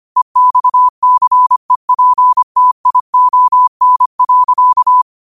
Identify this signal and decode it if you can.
Morse code, heard as a keyed tone, about 25 wpm, 1000 Hz. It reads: EXCEPTION.